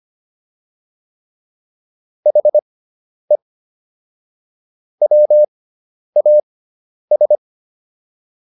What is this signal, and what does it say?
Morse code, heard as a keyed tone, about 25 wpm, spaced as for 10 wpm, 600 Hz. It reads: HE WAS